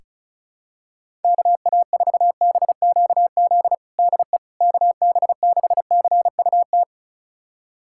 Morse code, heard as a keyed tone, about 36 wpm, 700 Hz. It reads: KA4BQZ DE KB6CUT
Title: USA Callsign DE Callsign - 6 Characters - 35wpm